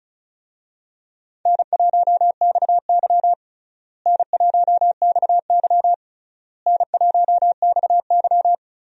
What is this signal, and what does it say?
Morse code, heard as a keyed tone, about 36 wpm, 700 Hz. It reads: N1XY N1XY N1XY
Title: USA Call Signs - 4 Characters - Repeated 3x - 35wpm